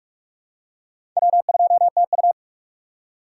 Morse code, heard as a keyed tone, about 46 wpm, 700 Hz. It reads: W2TU
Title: USA Call Signs - 4 Characters - 3x Character Spacing - 45wpm